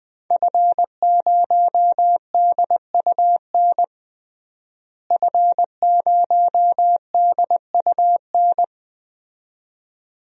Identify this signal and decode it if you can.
Morse code, heard as a keyed tone, about 21 wpm, 700 Hz. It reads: F0DUN F0DUN
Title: DXCC-100 Call Signs - 5 Characters - Repeated 2x - 20wpm